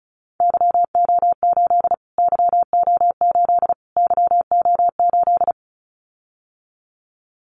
Morse code, heard as a keyed tone, about 34 wpm, 700 Hz. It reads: YO8 YO8 YO8
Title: Callsign Prefixes for Top 25 DXCC - Repeated 3x - 35wpm